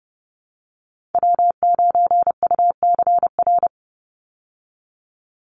Morse code, heard as a keyed tone, about 29 wpm, 700 Hz. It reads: W9UCR
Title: Call Signs Repeated 3x - 30wpm